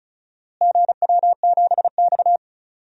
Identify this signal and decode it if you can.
Morse code, heard as a keyed tone, about 34 wpm, 700 Hz. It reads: GW7X
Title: DXCC-100 Call Signs - 4 Characters - Repeated 3x - 35wpm